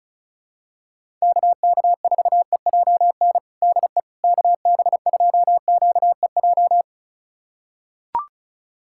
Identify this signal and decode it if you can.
Morse code, heard as a keyed tone, about 36 wpm, 700 Hz. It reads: KK4EJN DE KB2QEJ
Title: USA Callsign DE Callsign 35wpm